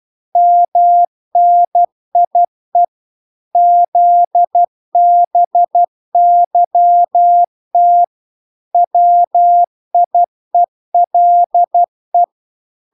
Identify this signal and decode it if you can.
Morse code, heard as a keyed tone, about 12 wpm, 700 Hz. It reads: MNIE ZBYT WIELE